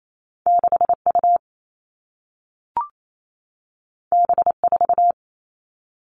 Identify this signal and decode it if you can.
Morse code, heard as a keyed tone, about 28 wpm, 700 Hz. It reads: 6U B4
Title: All Cut Numbers - 2-Digit 28wpm